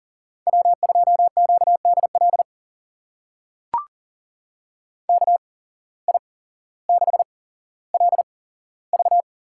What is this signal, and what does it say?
Morse code, heard as a keyed tone, about 40 wpm, spaced as for 17 wpm, 700 Hz. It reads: W2QDL KI6LV
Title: USA Call Signs - 8x Character Spacing - 40wpm